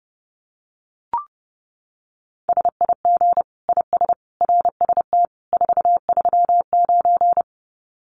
Morse code, heard as a keyed tone, about 30 wpm, 700 Hz. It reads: SIG IS RST 439
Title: Common QSO Phrases 30wpm